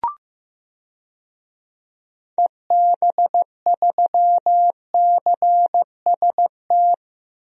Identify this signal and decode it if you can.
Morse code, heard as a keyed tone, about 16 wpm, 700 Hz. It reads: EB3CST
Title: DXCC-100 Call Signs - 6 Characters - Repeated 2x - 15wpm